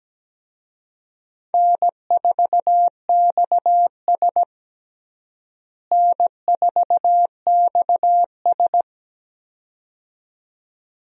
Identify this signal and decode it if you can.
Morse code, heard as a keyed tone, about 17 wpm, 700 Hz. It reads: N4XS N4XS
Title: USA Call Signs - 4 Characters - Repeated 2x - 17wpm